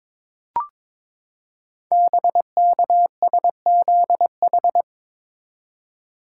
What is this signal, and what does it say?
Morse code, heard as a keyed tone, about 22 wpm, 700 Hz. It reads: BKSZH
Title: Code Groups - 5 Letters - 22wpm